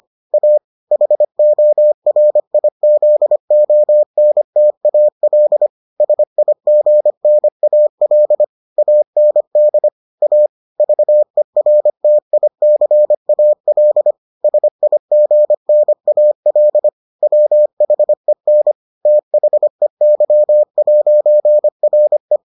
Morse code, heard as a keyed tone, about 25 wpm, 600 Hz. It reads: A HORIZONTAL SIGNAL AND A VERTICAL SIGNAL WHEN THEY'RE